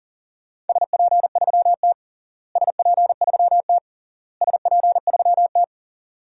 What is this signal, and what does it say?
Morse code, heard as a keyed tone, about 40 wpm, 700 Hz. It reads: SP3T SP3T SP3T
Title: DXCC-100 Call Signs - 4 Characters - Repeated 3x - 40wpm